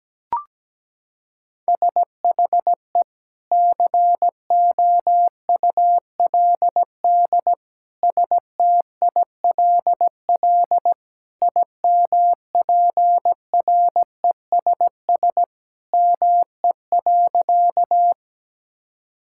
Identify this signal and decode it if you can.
Morse code, heard as a keyed tone, about 17 wpm, 700 Hz. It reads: SHE COULD STILL IMPRESS ME.
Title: Sentences from Top 1600 Words 17wpm